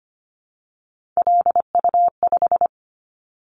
Morse code, heard as a keyed tone, about 25 wpm, 700 Hz. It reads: LU5
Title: Callsign Prefixes for Top 100 DXCC - 25wpm